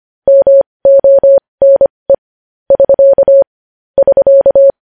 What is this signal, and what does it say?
Morse code, heard as a keyed tone, about 25 wpm, spaced as for 20 wpm, 550 Hz. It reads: MONE <SK> <SK>
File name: podcast / News headlines-1x-25x20